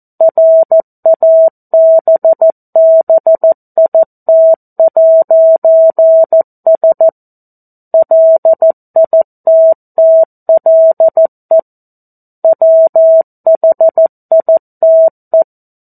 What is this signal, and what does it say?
Morse code, heard as a keyed tone, about 14 wpm, 650 Hz. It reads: RABBIT'S LITTLE WHITE
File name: 02-Chapter-02 14 WPM